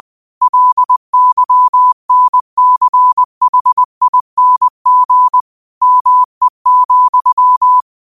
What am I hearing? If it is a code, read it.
Morse code, heard as a keyed tone, about 20 wpm, 1000 Hz. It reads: LYNCHING ME,
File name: 20wpm